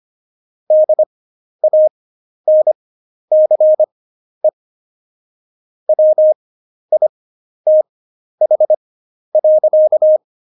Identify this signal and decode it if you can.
Morse code, heard as a keyed tone, about 25 wpm, spaced as for 11 wpm, 620 Hz. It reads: DANCE WITH.
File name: Jokes-25-char-10-eff